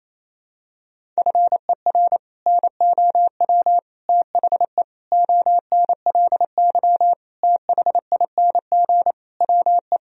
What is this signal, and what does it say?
Morse code, heard as a keyed tone, about 28 wpm, 700 Hz. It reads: FER NOW THE ONLY THING WE
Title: Sentences from Top 200 Words and Abbreviations 28wpm